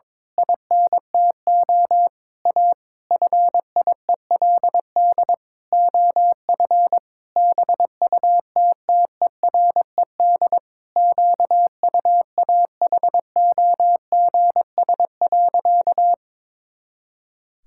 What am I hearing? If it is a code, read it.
Morse code, heard as a keyed tone, about 22 wpm, 700 Hz. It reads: INTO A FIELD OF BUTTERED QUAHOGS.
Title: Pangram Challenge v2 - 22wpm